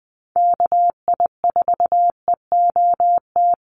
Morse code, heard as a keyed tone, about 20 wpm, 700 Hz. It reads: KI4EOT
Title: USA Call Signs Repeated 3x 20wpm